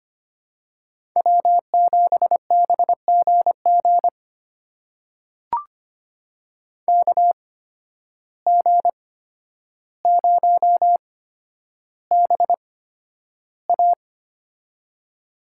Morse code, heard as a keyed tone, about 25 wpm, spaced as for 11 wpm, 700 Hz. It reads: W7BGG KG0BA
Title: USA Call Signs - 5 Characters - 8x Character Spacing - 25wpm